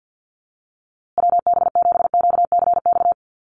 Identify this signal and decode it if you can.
Morse code, heard as a keyed tone, about 50 wpm, 700 Hz. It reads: WB7QCX
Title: USA Call Signs Repeated 2x 50wpm